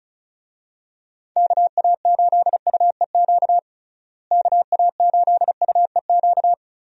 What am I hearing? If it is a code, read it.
Morse code, heard as a keyed tone, about 36 wpm, 700 Hz. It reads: KA8UEQ KA8UEQ
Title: USA Call Signs - 6 Characters - Repeated 3x - 35wpm